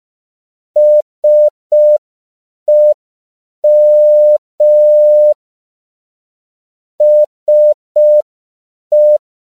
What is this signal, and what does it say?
Morse code, heard as a keyed tone, about 5 wpm, 600 Hz. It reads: SEM SE
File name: CW / 5PPM